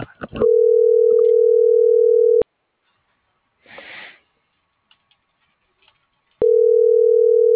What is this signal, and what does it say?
Ringback tone (North America), 440 + 480 Hz.